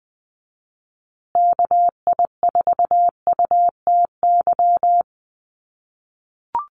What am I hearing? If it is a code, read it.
Morse code, heard as a keyed tone, about 20 wpm, 700 Hz. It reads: KI4UTY
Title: USA Call Signs Repeated 2x 20wpm